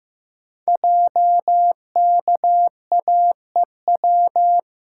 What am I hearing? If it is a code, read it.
Morse code, heard as a keyed tone, about 15 wpm, 700 Hz. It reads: JKAEW
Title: Code Groups - 5 Letters - 15wpm